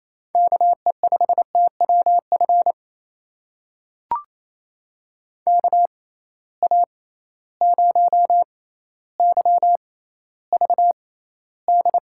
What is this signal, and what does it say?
Morse code, heard as a keyed tone, about 28 wpm, spaced as for 14 wpm, 700 Hz. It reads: KE5TWF KA0YVD
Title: USA Call Signs - 6 Characters - 6x Character Spacing - 28wpm